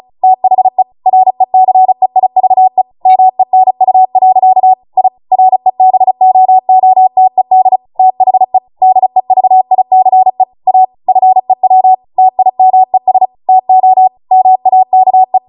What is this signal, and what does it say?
Morse code, heard as a keyed tone, about 36 wpm, 750 Hz. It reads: THE RECEIVE MENU. I REBOOTED THE DEVICE A FEW TIMES TO MAKE